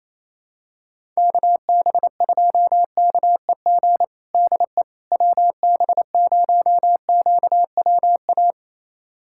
Morse code, heard as a keyed tone, about 28 wpm, 700 Hz. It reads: KB2KEG DE WB0QWA
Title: USA Callsign DE Callsign 28wpm